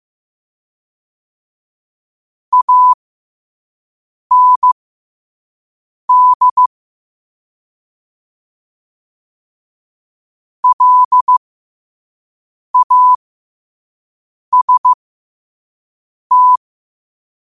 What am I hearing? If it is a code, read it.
Morse code, heard as a keyed tone, about 15 wpm, spaced as for 5 wpm, 1000 Hz. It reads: AND LAST
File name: qotdfiles / 5wpm